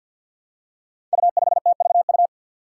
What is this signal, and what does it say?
Morse code, heard as a keyed tone, about 47 wpm, 700 Hz. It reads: U5TVU
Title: Five Challenging Letter-Number Combinations 50wpm